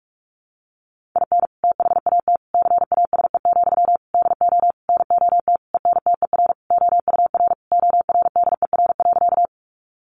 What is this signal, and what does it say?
Morse code, heard as a keyed tone, about 44 wpm, 700 Hz. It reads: IN THAT CASE, DO NOT ENTER OUR ORDER.
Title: Sentences from Top 1300 Words 45wpm